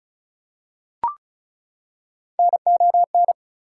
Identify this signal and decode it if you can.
Morse code, heard as a keyed tone, about 36 wpm, 700 Hz. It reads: NON
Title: Prefixes and Suffixes 35wpm